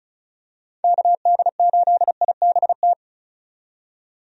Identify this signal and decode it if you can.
Morse code, heard as a keyed tone, about 36 wpm, 700 Hz. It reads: KD8IBT
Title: USA Call Signs - 6 Characters - 35wpm